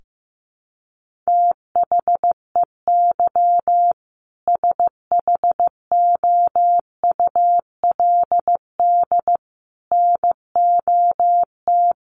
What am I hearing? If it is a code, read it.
Morse code, heard as a keyed tone, about 15 wpm, 700 Hz. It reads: THEY SHOULD NOT
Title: Sentences from Top 1500 Words 15wpm